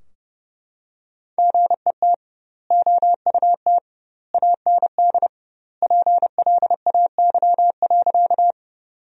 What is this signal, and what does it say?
Morse code, heard as a keyed tone, about 30 wpm, 700 Hz. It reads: GET OUT AND PLAY.
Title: 2x Word-Spacing - Sentences from Top 200 Words - 30wpm